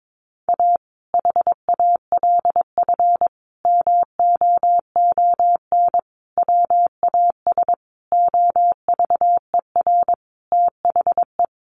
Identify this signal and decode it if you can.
Morse code, heard as a keyed tone, about 22 wpm, 700 Hz. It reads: A HALF MOON WAS OVER THE
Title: Sentences from Top 1100 Words 22wpm